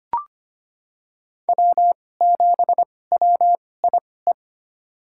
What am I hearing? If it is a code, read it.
Morse code, heard as a keyed tone, about 25 wpm, spaced as for 18 wpm, 700 Hz. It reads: W7WIE
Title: USA Call Signs - 2x Character Spacing - 25wpm